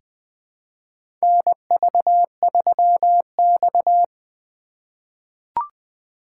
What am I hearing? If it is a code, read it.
Morse code, heard as a keyed tone, about 20 wpm, 700 Hz. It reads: NV3X
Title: USA Call Signs - 4 Characters - 8x Character Spacing - 20wpm